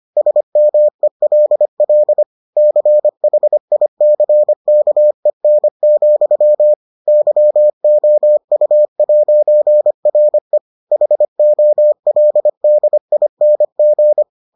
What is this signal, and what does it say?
Morse code, heard as a keyed tone, about 25 wpm, 600 Hz. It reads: SMELL CHICKEN, YOU'RE HOLDING